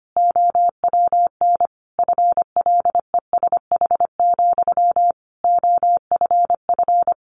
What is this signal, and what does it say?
Morse code, heard as a keyed tone, about 25 wpm, 700 Hz. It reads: OWN FLESH, OFF
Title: A Sound Of Thunder 25wpm